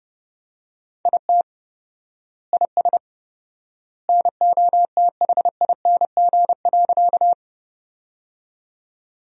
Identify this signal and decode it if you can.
Morse code, heard as a keyed tone, about 30 wpm, 700 Hz. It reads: IT IS NOTHING.
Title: 4x Word-Spacing - Sentences from Top 500 Words - 30wpm